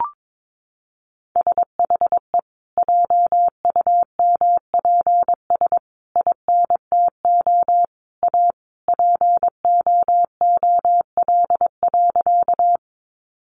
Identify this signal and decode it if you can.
Morse code, heard as a keyed tone, about 22 wpm, 700 Hz. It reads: SHE JUMPS INTO A POOL.
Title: Additional Easy Phrases 22wpm